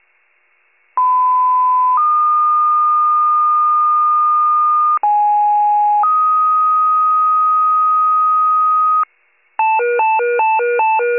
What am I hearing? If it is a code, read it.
Two-tone page, heard as a one-second tone, then a three-second tone, about 1010 then 1220 Hz, about 830 then 1250 Hz. It is heard from 1.0 to 9.0 s.